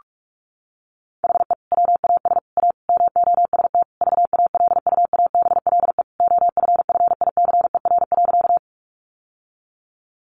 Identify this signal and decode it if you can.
Morse code, heard as a keyed tone, about 44 wpm, 700 Hz. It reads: HE WAS A MOST VALUABLE OFFICER.
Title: Sentences from Top 1200 Words 45wpm